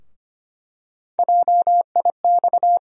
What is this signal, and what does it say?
Morse code, heard as a keyed tone, about 25 wpm, 700 Hz. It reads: JIX
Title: USA Call Sign Postfixes - 3 Letters 25wpm